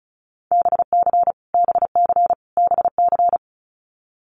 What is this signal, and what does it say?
Morse code, heard as a keyed tone, about 34 wpm, 700 Hz. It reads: BC BC BC
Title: Canadian Provinces and Territories - Repeated 3x - 35wpm